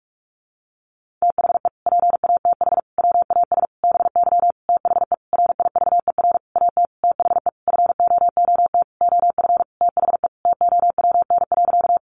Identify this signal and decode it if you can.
Morse code, heard as a keyed tone, about 44 wpm, 700 Hz. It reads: THE PATH WAS BY THE RIVER AT THE FOOT OF THE TOWN.
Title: Sentences from Top 1400 Words 45wpm